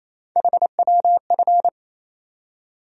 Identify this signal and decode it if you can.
Morse code, heard as a keyed tone, about 27 wpm, 700 Hz. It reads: HWF
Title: USA Call Sign Postfixes - 3 Letters 28wpm